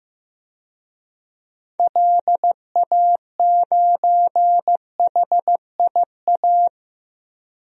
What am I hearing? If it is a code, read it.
Morse code, heard as a keyed tone, about 15 wpm, 700 Hz. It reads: LA9HIA
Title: DXCC-100 Call Signs - 6 Characters - Repeated 3x - 15wpm